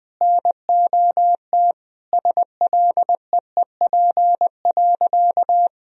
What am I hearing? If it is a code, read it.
Morse code, heard as a keyed tone, about 20 wpm, 700 Hz. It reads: NOT SLEEP.
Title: Sentences from Top 1000 Words - Review Part 3 - 20wpm